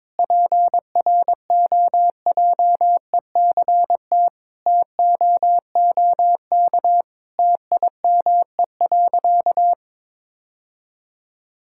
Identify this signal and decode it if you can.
Morse code, heard as a keyed tone, about 22 wpm, 700 Hz. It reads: PROJECT TOOK TIME.
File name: practice / 22wpm / Sentences with 3-Word Phrases - 22wpm